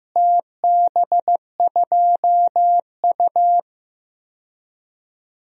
Morse code, heard as a keyed tone, about 15 wpm, 700 Hz. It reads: TB2U
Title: DXCC-100 Call Signs - 4 Characters - Repeated 2x - 15wpm